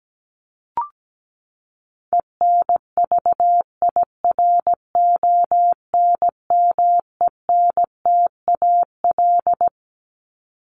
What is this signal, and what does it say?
Morse code, heard as a keyed tone, about 17 wpm, 700 Hz. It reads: ENVIRONMENTAL
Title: Top 1001-1250 Words 17wpm